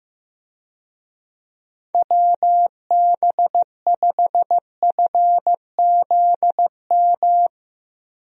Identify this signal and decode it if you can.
Morse code, heard as a keyed tone, about 15 wpm, 700 Hz. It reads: WB5FZM
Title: USA Call Signs - 6 Characters - Repeated 3x - 15wpm